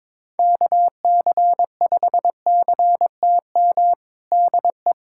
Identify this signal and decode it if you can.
Morse code, heard as a keyed tone, about 22 wpm, 700 Hz. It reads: KC5CTM DE
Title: USA Callsign DE Callsign - 6 Characters - 22wpm